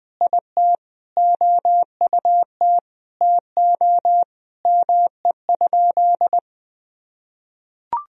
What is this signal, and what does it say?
Morse code, heard as a keyed tone, about 21 wpm, 700 Hz. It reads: IT OUT TO ME?
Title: Sentences from Top 200 Words and Abbreviations 20wpm